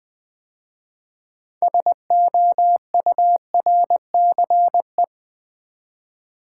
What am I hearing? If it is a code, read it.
Morse code, heard as a keyed tone, about 20 wpm, 700 Hz. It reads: SOURCE
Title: 6 Letter Words - First 3 Letters Given - 20wpm